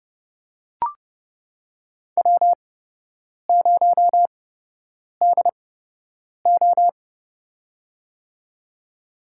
Morse code, heard as a keyed tone, about 30 wpm, spaced as for 8 wpm, 700 Hz. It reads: W0DO
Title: USA Call Signs - 8x Character Spacing - 30wpm